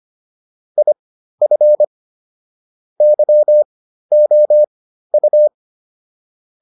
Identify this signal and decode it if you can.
Morse code, heard as a keyed tone, about 25 wpm, spaced as for 13 wpm, 600 Hz. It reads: IF YOU